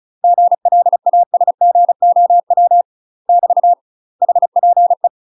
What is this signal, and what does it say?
Morse code, heard as a keyed tone, about 34 wpm, 700 Hz. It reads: GLASGOW <BT> HPE
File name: UK MASTER QSOs TRAINING / 35WPM / UK MASTER QSOs 0024